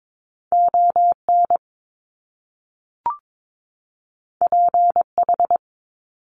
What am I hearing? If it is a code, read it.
Morse code, heard as a keyed tone, about 22 wpm, 700 Hz. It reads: ON PH
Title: Top Two Letter Combinations in Words 22wpm